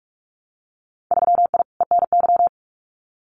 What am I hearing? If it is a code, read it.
Morse code, heard as a keyed tone, about 46 wpm, 690 Hz. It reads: 3I ENY
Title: Winter Field Day Exchanges Encore 45wpm